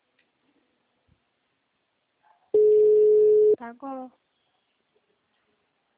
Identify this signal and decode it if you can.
Ringback tone (Europe, 425 Hz), with speech.